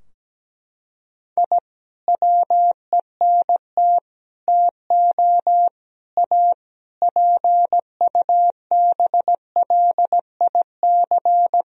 Morse code, heard as a keyed tone, about 17 wpm, 700 Hz. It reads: I WENT TO A PUBLIC
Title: Sentences from Top 1000 Words - Review Part 3 - 17wpm